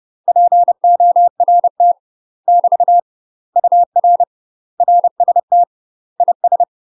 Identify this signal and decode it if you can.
Morse code, heard as a keyed tone, about 30 wpm, 700 Hz. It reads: PORT <BT> UR RST IS
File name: UK MASTER QSOs TRAINING / 30WPM / UK MASTER QSOs 0047